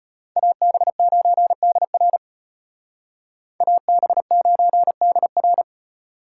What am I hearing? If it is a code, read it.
Morse code, heard as a keyed tone, about 29 wpm, 700 Hz. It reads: AB9DR AB9DR
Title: Speed Racing - USA 2x2 Call Signs 25wpm